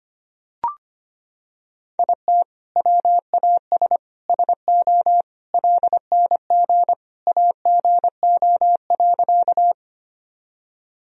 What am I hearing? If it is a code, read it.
Morse code, heard as a keyed tone, about 25 wpm, 700 Hz. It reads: IT WAS SO LNG AGO.
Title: Sentences from Top 500 Words and Abbreviations 25wpm